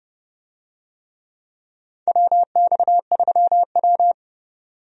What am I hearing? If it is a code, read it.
Morse code, heard as a keyed tone, about 31 wpm, 700 Hz. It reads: WX3W
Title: USA Call Signs - 4 Characters - 30wpm